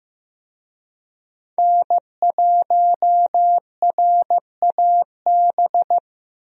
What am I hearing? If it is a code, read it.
Morse code, heard as a keyed tone, about 15 wpm, 700 Hz. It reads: N1RAB
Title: USA Call Signs - 5 Characters - Repeated 2x - 15wpm